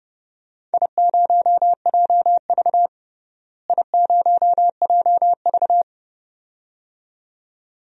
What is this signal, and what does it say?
Morse code, heard as a keyed tone, about 30 wpm, 700 Hz. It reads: I0JV I0JV
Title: DXCC-100 Call Signs - 4 Characters - Repeated 2x - 30wpm